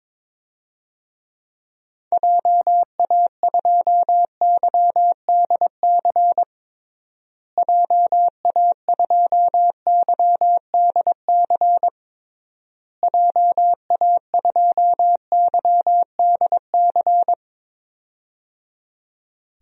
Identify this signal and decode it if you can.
Morse code, heard as a keyed tone, about 22 wpm, 700 Hz. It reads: JA2YDC JA2YDC JA2YDC
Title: DXCC-100 Call Signs - 6 Characters - Repeated 3x - 22wpm